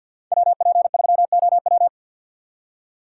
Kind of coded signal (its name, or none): Morse code